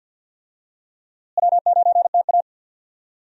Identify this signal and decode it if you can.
Morse code, heard as a keyed tone, about 51 wpm, 700 Hz. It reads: W9TA
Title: USA Call Signs - 4 Characters - Repeated 3x - 50wpm